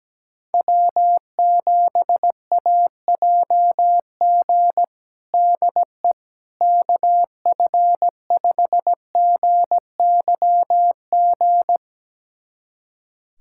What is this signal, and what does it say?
Morse code, heard as a keyed tone, about 17 wpm, 700 Hz. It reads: W7AJG DE KF5GYG
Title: USA Callsign DE Callsign 17wpm